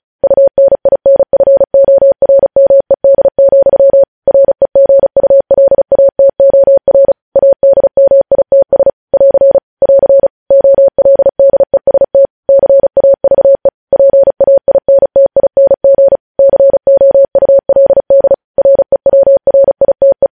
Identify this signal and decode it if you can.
Morse code, heard as a keyed tone, about 36 wpm, 550 Hz. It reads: UNINFORMED, REGULATOR ADMITS <AR> <AR> OLDEST CAVE PAINTING COULD REWRITE
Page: News Headlines in Morse Code at 35 WPM